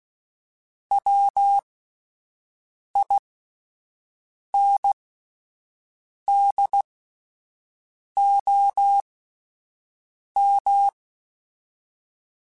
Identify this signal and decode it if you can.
Morse code, heard as a keyed tone, about 16 wpm, spaced as for 5 wpm, 800 Hz. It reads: WINDOM
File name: fromSMs / FromSMs mp3s / Antennas@5